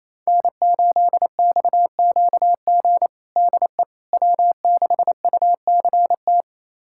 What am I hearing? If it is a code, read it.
Morse code, heard as a keyed tone, about 28 wpm, 700 Hz. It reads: N8XQG DE W6UCT
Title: USA Callsign DE Callsign - 5 Characters - 28wpm